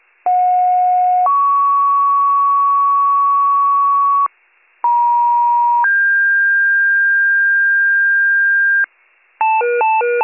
Two-tone page, heard as a one-second tone, then a three-second tone, about 730 then 1120 Hz, about 950 then 1600 Hz, with noise. It is heard from 0.3 to 8.9 s.